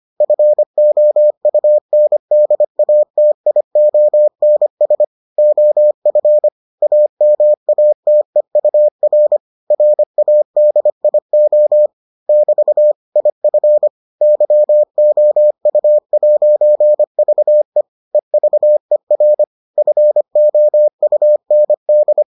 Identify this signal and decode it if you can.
Morse code, heard as a keyed tone, about 25 wpm, 600 Hz. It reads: FOUNDATIONS OF AMATEUR RADIO <BT> IF YOU'VE EVER FOUND